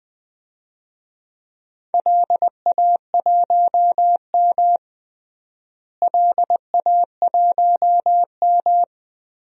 Morse code, heard as a keyed tone, about 20 wpm, 700 Hz. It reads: LA1M LA1M
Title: DXCC-100 Call Signs - 4 Characters - Repeated 3x - 20wpm